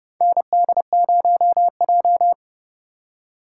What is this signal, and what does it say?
Morse code, heard as a keyed tone, about 30 wpm, 700 Hz. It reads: ND0J
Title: USA Call Signs - 3x Character Spacing - 30wpm